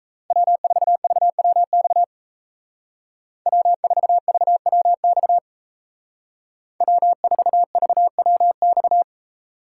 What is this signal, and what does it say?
Morse code, heard as a keyed tone, about 37 wpm, 700 Hz. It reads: W4VWX W4VWX W4VWX